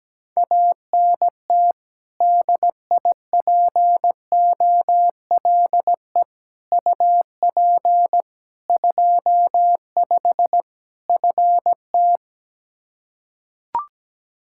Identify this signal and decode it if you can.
Morse code, heard as a keyed tone, about 17 wpm, 700 Hz. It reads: ANT DIPOLE UP 25 FT
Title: Practice QSO - Exchange 2 - Info - 17wpm